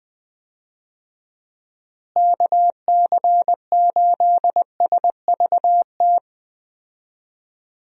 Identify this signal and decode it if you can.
Morse code, heard as a keyed tone, about 21 wpm, 700 Hz. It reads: KC8SVT
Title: USA Call Signs - 6 Characters - Repeated 3x - 20wpm